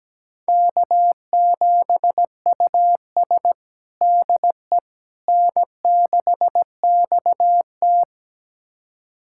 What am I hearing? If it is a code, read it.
Morse code, heard as a keyed tone, about 17 wpm, 700 Hz. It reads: K7US DE N6XT
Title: USA Callsign DE Callsign - 4 Characters - 17wpm